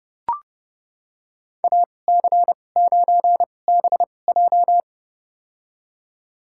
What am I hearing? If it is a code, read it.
Morse code, heard as a keyed tone, about 32 wpm, spaced as for 22 wpm, 700 Hz. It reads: AC9BJ
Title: USA Call Signs - 5 Characters - 2x Character Spacing - 30wpm